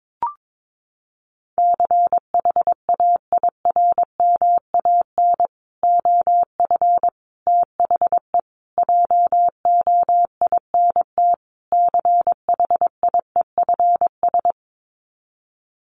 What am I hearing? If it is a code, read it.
Morse code, heard as a keyed tone, about 22 wpm, 700 Hz. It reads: CHAIRMAN OF THE JOINT CHIEFS
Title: Sets of 5 Words Encore 22wpm